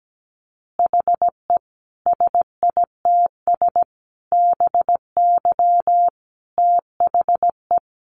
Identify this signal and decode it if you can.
Morse code, heard as a keyed tone, about 17 wpm, 700 Hz. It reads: HE SITS BY THE
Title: Additional Easy Phrases 17wpm